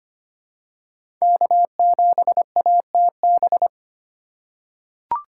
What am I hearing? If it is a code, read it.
Morse code, heard as a keyed tone, about 25 wpm, 700 Hz. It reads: K7ATB
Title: USA Call Signs - 2x Character Spacing - 25wpm